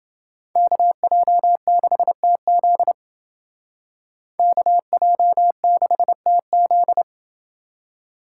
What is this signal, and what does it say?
Morse code, heard as a keyed tone, about 28 wpm, 700 Hz. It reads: KJ6TZ KJ6TZ